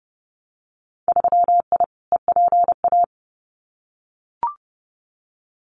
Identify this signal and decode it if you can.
Morse code, heard as a keyed tone, about 30 wpm, 700 Hz. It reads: 3I EPA
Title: Winter Field Day Exchanges 30wpm